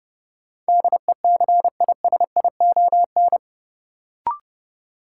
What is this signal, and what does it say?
Morse code, heard as a keyed tone, about 30 wpm, 700 Hz. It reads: DECISION
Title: Sentences from Top 1000 Words - New Word Drill - 30wpm